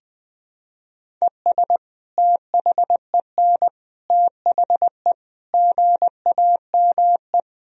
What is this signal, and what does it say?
Morse code, heard as a keyed tone, about 20 wpm, 700 Hz. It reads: ES THEN THE GAME
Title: Sentences from Top 500 Words and Abbreviations 20wpm